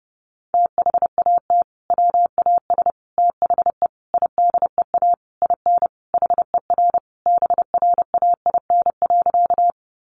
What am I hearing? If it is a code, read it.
Morse code, heard as a keyed tone, about 30 wpm, 700 Hz. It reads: THAT WAS THE IDEA IN HER BRAIN.
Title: Sentences from Top 1100 Words 30wpm